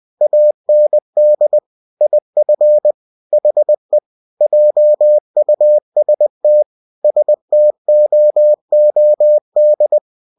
Morse code, heard as a keyed tone, about 20 wpm, 600 Hz. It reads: AND IF HE JUST STOOD
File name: cw-books / winnie-the-pooh / winnie-20wpm-page-0003